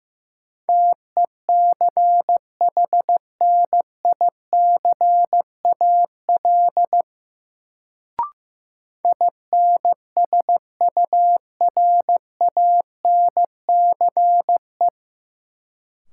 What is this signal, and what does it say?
Morse code, heard as a keyed tone, about 15 wpm, 700 Hz. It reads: TECHNICAL INSURANCE
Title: Sentences from Top 1100 Words - New Word Drill Alt - 15wpm